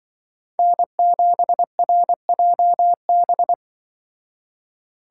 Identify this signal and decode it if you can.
Morse code, heard as a keyed tone, about 25 wpm, 700 Hz. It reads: N7RJB